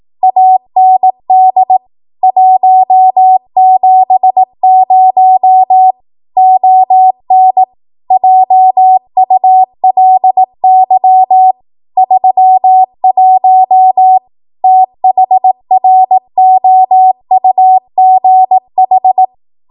Morse code, heard as a keyed tone, about 18 wpm, 750 Hz. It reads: AND 170 ON JULY 31 THROUGH
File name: Archive / 18 WPM / 18WPM